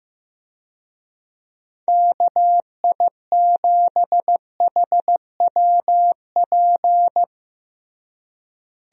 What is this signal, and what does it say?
Morse code, heard as a keyed tone, about 15 wpm, 700 Hz. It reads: KI7HWP
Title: USA Call Signs - 6 Characters - Repeated 3x - 15wpm